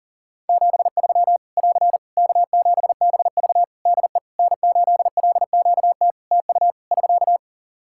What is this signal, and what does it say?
Morse code, heard as a keyed tone, about 40 wpm, 700 Hz. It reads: 73 <AR> K7BV DE N8LQT TU <SK>
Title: Practice QSO - Exchange 4 - Ending - 40wpm